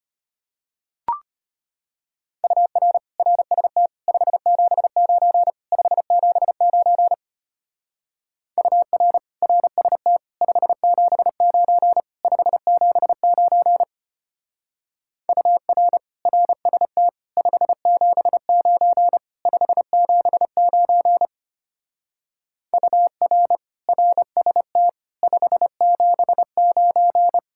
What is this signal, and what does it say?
Morse code, heard as a keyed tone, about 30 wpm, 700 Hz. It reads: UR RST 579 579 UR RST 579 579 UR RST 579 579 UR RST 579
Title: Speed-Racing Common QSO Phrases - 25wpm